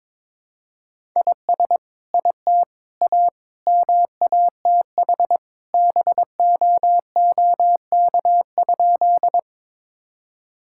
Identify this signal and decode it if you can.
Morse code, heard as a keyed tone, about 22 wpm, 700 Hz. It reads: IS IT A MATH BOOK?
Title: Sentences from Top 1000 Words - Review Part 6 - 22wpm